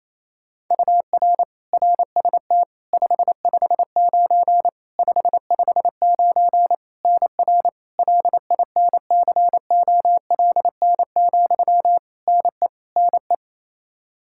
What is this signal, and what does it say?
Morse code, heard as a keyed tone, about 28 wpm, 700 Hz. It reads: UR RST 559 559 NR LINCOLN, NE NE
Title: Practice QSO - Exchange 1 - Beginning - 28wpm